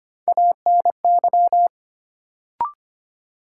Morse code, heard as a keyed tone, about 25 wpm, 700 Hz. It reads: ANY
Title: Sets of 3 Words - New Word Drill - 25wpm